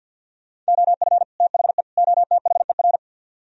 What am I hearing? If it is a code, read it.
Morse code, heard as a keyed tone, about 66 wpm, spaced as for 58 wpm, 700 Hz. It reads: OF THE OTHER
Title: Speed-Racing - Sets of 3 Words 50wpm